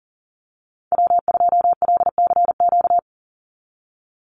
Morse code, heard as a keyed tone, about 40 wpm, 700 Hz. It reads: W2LCQ
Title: Call Signs - 40wpm